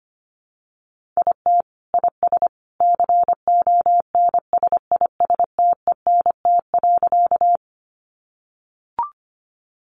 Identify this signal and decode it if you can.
Morse code, heard as a keyed tone, about 25 wpm, 700 Hz. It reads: IT IS CONSISTENT.
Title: Sentences from Top 1500 Words 25wpm